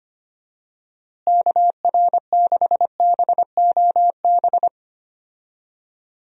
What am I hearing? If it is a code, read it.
Morse code, heard as a keyed tone, about 25 wpm, 700 Hz. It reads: KR6BOB
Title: USA Call Signs - 6 Characters - Repeated 3x - 25wpm